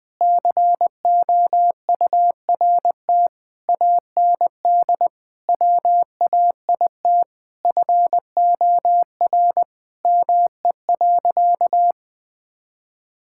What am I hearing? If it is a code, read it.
Morse code, heard as a keyed tone, about 20 wpm, 700 Hz. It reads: COURT AND WAIT FOR ME.
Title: Sentences from Top 1600 Words 20wpm